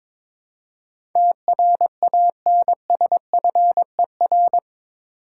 Morse code, heard as a keyed tone, about 22 wpm, 700 Hz. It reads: TRANSFER
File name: practice / 22wpm / Eight Letter Words - 22wpm